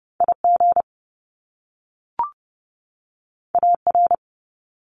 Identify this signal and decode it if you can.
Morse code, heard as a keyed tone, about 30 wpm, 700 Hz. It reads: IG AR
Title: Top Two Letter Combinations in Words 30wpm